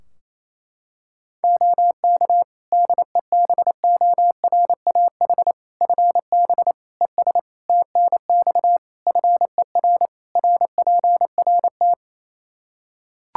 Morse code, heard as a keyed tone, about 28 wpm, 700 Hz. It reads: OK DEBORAH FB ES TNX FER RPRT
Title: Practice QSO - Exchange 2 - Info - 28wpm